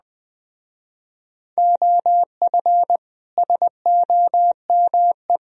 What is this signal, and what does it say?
Morse code, heard as a keyed tone, about 20 wpm, 700 Hz. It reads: OF SOME